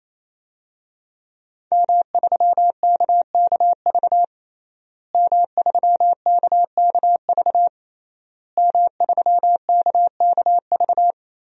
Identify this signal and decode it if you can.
Morse code, heard as a keyed tone, about 28 wpm, 700 Hz. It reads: M3KKV M3KKV M3KKV
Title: DXCC-100 Call Signs - 5 Characters - Repeated 3x - 28wpm